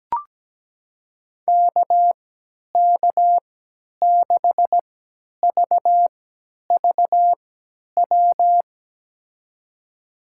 Morse code, heard as a keyed tone, about 17 wpm, spaced as for 10 wpm, 700 Hz. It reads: KK6VVW
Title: USA Call Signs - 6 Characters - 3x Character Spacing - 17wpm